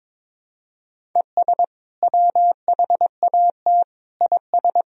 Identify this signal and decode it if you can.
Morse code, heard as a keyed tone, about 22 wpm, 700 Hz. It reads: ES WHAT IS
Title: Sentences from Top 100 Words and Abbreviations 22wpm